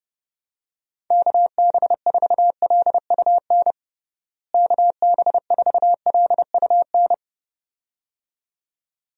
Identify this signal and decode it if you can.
Morse code, heard as a keyed tone, about 30 wpm, 700 Hz. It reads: KB4LUN KB4LUN
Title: USA Call Signs - 6 Characters - Repeated 2x - 30wpm